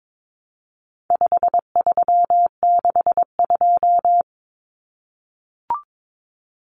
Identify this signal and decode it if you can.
Morse code, heard as a keyed tone, about 22 wpm, 700 Hz. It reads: 5362
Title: Common Cut Numbers - 4-Digit 22wpm